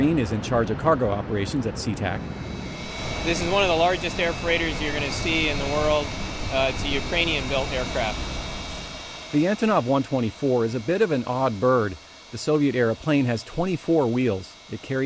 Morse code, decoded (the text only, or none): none